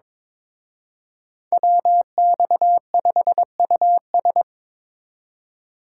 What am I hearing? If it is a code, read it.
Morse code, heard as a keyed tone, about 25 wpm, 700 Hz. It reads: WX5US
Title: Speed Racing - USA 2x2 Call Signs 22wpm